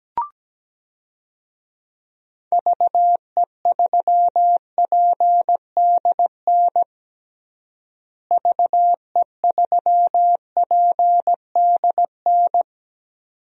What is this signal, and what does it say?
Morse code, heard as a keyed tone, about 17 wpm, 700 Hz. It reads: VE3PDN VE3PDN
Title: DXCC-100 Call Signs - 6 Characters - Repeated 3x - 17wpm